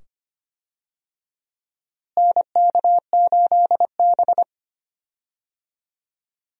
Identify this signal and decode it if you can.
Morse code, heard as a keyed tone, about 25 wpm, 700 Hz. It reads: NK8B
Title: USA Call Signs - 4 Characters - Repeated 2x - 25wpm